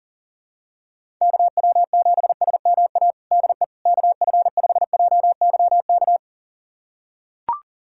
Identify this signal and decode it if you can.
Morse code, heard as a keyed tone, about 40 wpm, 700 Hz. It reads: KW7SMA DE KF5JYK
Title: USA Callsign DE Callsign 40wpm